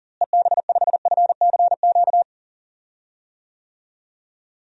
Morse code, heard as a keyed tone, about 40 wpm, 700 Hz. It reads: EB5FCQ
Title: DXCC-100 Call Signs - 6 Characters - Repeated 2x - 40wpm